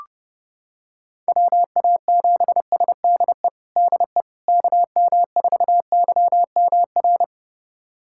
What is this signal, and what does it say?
Morse code, heard as a keyed tone, about 32 wpm, 700 Hz. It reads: WA7SDE DE KM4YMR
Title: USA Callsign DE Callsign - 6 Characters - 30wpm